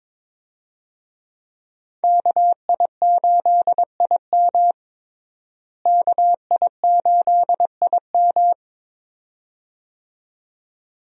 Morse code, heard as a keyed tone, about 22 wpm, 700 Hz. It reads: KI8IM KI8IM
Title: USA Call Signs - 5 Characters - Repeated 2x - 22wpm